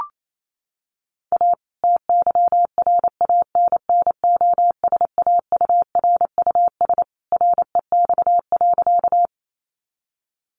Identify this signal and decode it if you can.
Morse code, heard as a keyed tone, about 28 wpm, 700 Hz. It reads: A TYRANNOSAURUS REX.
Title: A Sound Of Thunder 28wpm